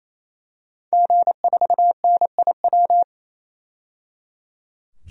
Morse code, heard as a keyed tone, about 28 wpm, 700 Hz. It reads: G4NIW
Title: DXCC-100 Call Signs - 5 Characters - 28wpm